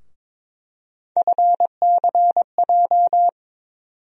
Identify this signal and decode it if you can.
Morse code, heard as a keyed tone, about 22 wpm, 700 Hz. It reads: FCJ
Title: USA Call Sign Postfixes - 3 Letters 22wpm